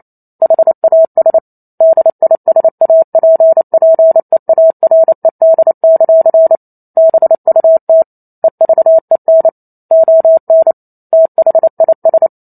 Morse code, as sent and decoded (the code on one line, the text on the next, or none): .... .- ... / -.. .. ... .- .--. .--. .- .-. -.. -.-.-. / -... ..- - / . ...- . -. / --- -. / - .... .. ...
HAS DISAPPEARED; BUT EVEN ON THIS